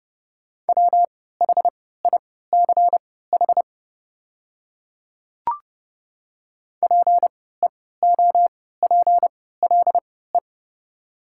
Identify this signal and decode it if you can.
Morse code, heard as a keyed tone, about 29 wpm, spaced as for 17 wpm, 690 Hz. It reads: WHICH PEOPLE
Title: 3x Character Spacing - Top 200 Words - 30wpm